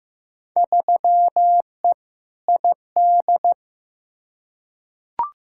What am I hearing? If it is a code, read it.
Morse code, heard as a keyed tone, about 15 wpm, 700 Hz. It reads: 3E ID
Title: Field Day Exchanges Encore - 15wpm